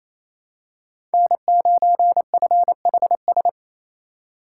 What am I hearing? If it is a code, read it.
Morse code, heard as a keyed tone, about 28 wpm, 700 Hz. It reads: N9FHS
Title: USA Call Signs - 5 Characters - 4x Character Spacing - 28wpm